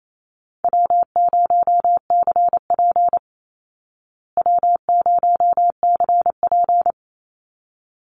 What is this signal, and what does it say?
Morse code, heard as a keyed tone, about 28 wpm, 700 Hz. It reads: W0CP W0CP
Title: Easy Call Signs Repeated 2x - 28wpm